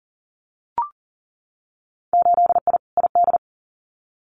Morse code, heard as a keyed tone, about 41 wpm, 700 Hz. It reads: 8I ID
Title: Winter Field Day Exchanges Encore 40wpm